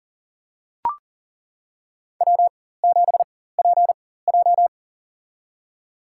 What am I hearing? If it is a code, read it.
Morse code, heard as a keyed tone, about 40 wpm, spaced as for 19 wpm, 700 Hz. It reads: W7PJ
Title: USA Call Signs - 4 Characters - 4x Character Spacing - 40wpm